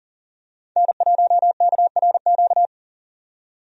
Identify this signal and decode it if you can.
Morse code, heard as a keyed tone, about 40 wpm, 700 Hz. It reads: N1KRQ
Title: USA Call Signs - 5 Characters - 40wpm